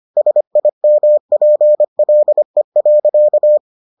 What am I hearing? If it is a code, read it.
Morse code, heard as a keyed tone, about 25 wpm, 600 Hz. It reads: SIMPLE.